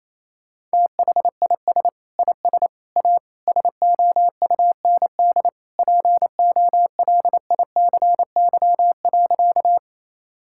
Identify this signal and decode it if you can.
Morse code, heard as a keyed tone, about 28 wpm, 700 Hz. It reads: THIS IS A SOUND POLICY.
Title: Sentences from Top 1000 Words - Review Part 5 - 28wpm